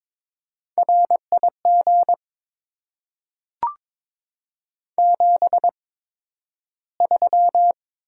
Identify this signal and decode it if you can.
Morse code, heard as a keyed tone, about 22 wpm, spaced as for 12 wpm, 700 Hz. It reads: RIG 73
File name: Top 20 QSO Elements - 8x Character Spacing - 22wpm